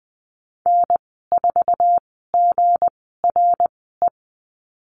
Morse code, heard as a keyed tone, about 20 wpm, spaced as for 15 wpm, 700 Hz. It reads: N4GRE
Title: 161 Lesson 26g - Letter g - Call Signs 20wpm